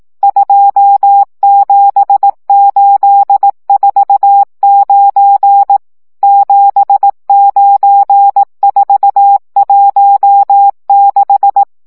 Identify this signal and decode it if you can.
Morse code, heard as a keyed tone, about 18 wpm, 800 Hz. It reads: 27849 79416
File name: morze vizsga